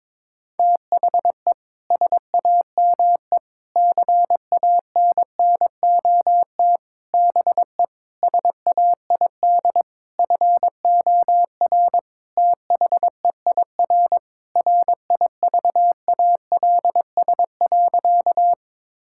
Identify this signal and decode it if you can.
Morse code, heard as a keyed tone, about 22 wpm, 700 Hz. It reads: THE SAME CANNOT BE SAID FOR THEIR RIVALS.
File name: Sentences with 7-Word Phrases - 22wpm